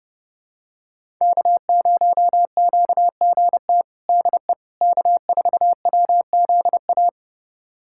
Morse code, heard as a keyed tone, about 30 wpm, 700 Hz. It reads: K0QGT DE K4WZA
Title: USA Callsign DE Callsign - 5 Characters - 30wpm